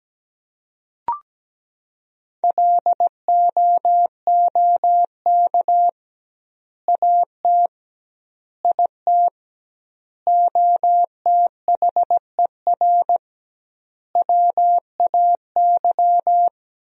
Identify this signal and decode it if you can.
Morse code, heard as a keyed tone, about 17 wpm, 700 Hz. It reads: LOOK AT IT OTHER WAY
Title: 2x Word-Spacing - Sentences from Top 200 Words - 17wpm